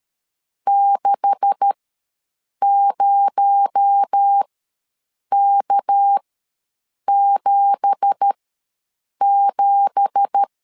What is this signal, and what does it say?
Morse code, heard as a keyed tone, about 13 wpm, spaced as for 7 wpm, 800 Hz. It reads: 60K77
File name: LEZIONI / 9 2867 ws 12wpm 3431